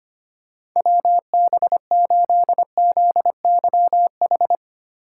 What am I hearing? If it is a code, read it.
Morse code, heard as a keyed tone, about 25 wpm, 690 Hz. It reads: WB8ZYH
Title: USA Call Signs - 6 Characters - 3x Character Spacing - 25wpm